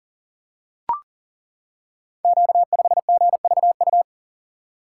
Morse code, heard as a keyed tone, about 40 wpm, 690 Hz. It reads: Q5GVU
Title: Code Groups - 5 Letter:Numbers - 40wpm